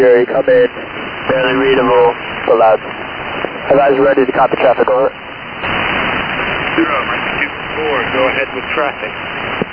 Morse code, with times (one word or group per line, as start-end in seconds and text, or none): none